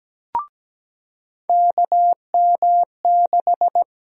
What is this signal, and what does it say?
Morse code, heard as a keyed tone, about 17 wpm, 700 Hz. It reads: KM6
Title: USA Call Sign Prefixes - 17wpm